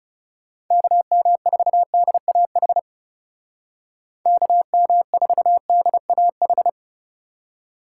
Morse code, heard as a keyed tone, about 37 wpm, 700 Hz. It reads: KM4DAH KM4DAH
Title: Speed Racing - USA 2x3 Call Signs 30wpm